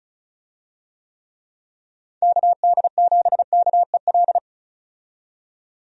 Morse code, heard as a keyed tone, about 34 wpm, 700 Hz. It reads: KD7KEL
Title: USA Call Signs - 6 Characters - 35wpm